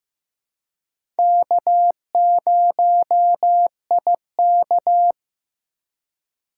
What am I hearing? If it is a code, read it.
Morse code, heard as a keyed tone, about 15 wpm, 700 Hz. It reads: K0IK
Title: USA Call Signs - 4 Characters - 2x Character Spacing - 15wpm